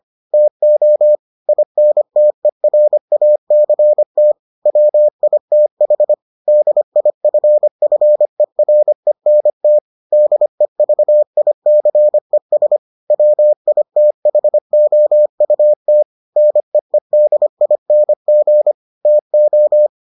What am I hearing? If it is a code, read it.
Morse code, heard as a keyed tone, about 25 wpm, 600 Hz. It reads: TO INTERACT WITH DIFFERENT DEVICES WITHOUT NEEDING TO